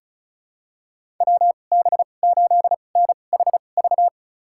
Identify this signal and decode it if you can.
Morse code, heard as a keyed tone, about 36 wpm, spaced as for 25 wpm, 700 Hz. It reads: WB8NHV
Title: USA Call Signs - 6 Characters - 2x Character Spacing - 35wpm